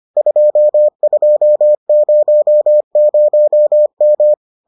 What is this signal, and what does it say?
Morse code, heard as a keyed tone, about 25 wpm, 600 Hz. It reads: 2200M